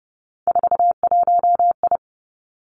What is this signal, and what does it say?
Morse code, heard as a keyed tone, about 30 wpm, 700 Hz. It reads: 41I
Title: Three Challenging Letter-Number Combinations 30wpm